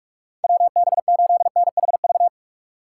Morse code, heard as a keyed tone, about 47 wpm, 700 Hz. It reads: WB8NHV
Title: USA Call Signs - 6 Characters - 4x Character Spacing - 45wpm